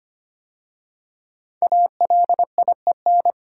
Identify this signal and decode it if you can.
Morse code, heard as a keyed tone, about 25 wpm, 700 Hz. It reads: ALIEN
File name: 5 Letter Words - First 3 Letters Given - 25wpm